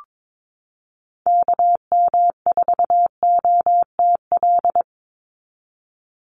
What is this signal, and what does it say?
Morse code, heard as a keyed tone, about 22 wpm, 700 Hz. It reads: KM4OTL
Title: USA Call Signs 22wpm